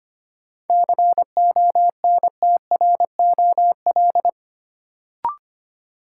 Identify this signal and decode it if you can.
Morse code, heard as a keyed tone, about 25 wpm, 700 Hz. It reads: CONTROL
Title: Sentences from Top 1000 Words - Review Part 2 - New Word Drill - 25wpm